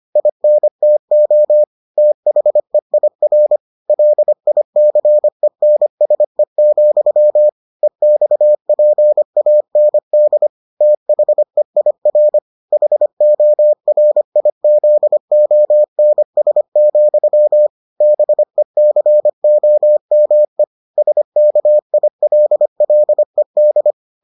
Morse code, heard as a keyed tone, about 25 wpm, 600 Hz. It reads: INTO THEIR LICENSE, EXPAND THEIR HORIZONS, BECOME SKILLED